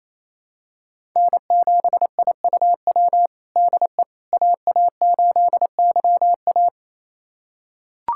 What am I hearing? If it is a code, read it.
Morse code, heard as a keyed tone, about 27 wpm, 700 Hz. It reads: N7IUW DE AA8YA
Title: USA Callsign DE Callsign - 5 Characters - 28wpm